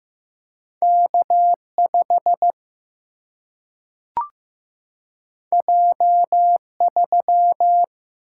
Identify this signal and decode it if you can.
Morse code, heard as a keyed tone, about 15 wpm, 700 Hz. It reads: K5 J3
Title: Single-Letter Single-Number 15wpm